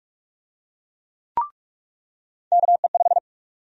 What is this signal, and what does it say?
Morse code, heard as a keyed tone, about 46 wpm, 700 Hz. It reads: KE5
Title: USA Call Sign Prefixes - 45wpm